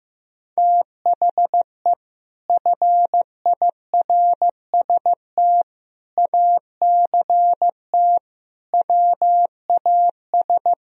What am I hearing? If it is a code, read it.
Morse code, heard as a keyed tone, about 15 wpm, 700 Hz. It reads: THE FIRST ACT WAS
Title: Sentences from Top 1000 Words - Review Part 1 - 15wpm